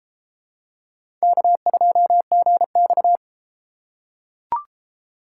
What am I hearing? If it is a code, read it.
Morse code, heard as a keyed tone, about 30 wpm, 700 Hz. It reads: K2GX